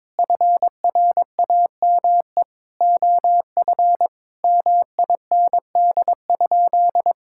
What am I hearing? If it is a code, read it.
Morse code, heard as a keyed tone, about 22 wpm, 700 Hz. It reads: FRAME OF MIND?
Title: Sentences from Top 1000 Words - Review Part 6 - 22wpm